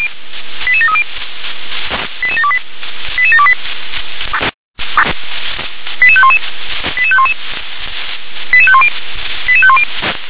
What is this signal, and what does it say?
Five-tone selective call (ZVEI), heard as quick runs of five tones, tones of about 70 ms. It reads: 88730 88728 88728 88511 88511 88510 88510